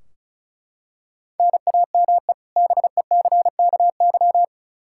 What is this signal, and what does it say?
Morse code, heard as a keyed tone, about 36 wpm, 700 Hz. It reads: NAME BECKY
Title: K1USN SST QSO Practice 35wpm